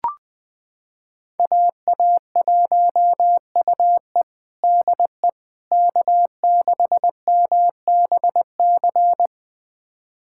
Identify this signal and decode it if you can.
Morse code, heard as a keyed tone, about 20 wpm, 700 Hz. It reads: AA1UE DE K6MBC
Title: USA Callsign DE Callsign - 5 Characters - 20wpm